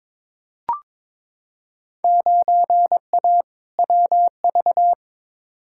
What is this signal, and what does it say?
Morse code, heard as a keyed tone, about 22 wpm, 700 Hz. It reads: 9A WV
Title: Field Day Exchanges - 22wpm